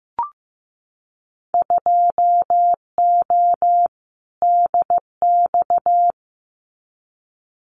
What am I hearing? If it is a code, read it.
Morse code, heard as a keyed tone, about 15 wpm, 700 Hz. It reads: 2O DX
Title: Winter Field Day Exchanges Encore 15wpm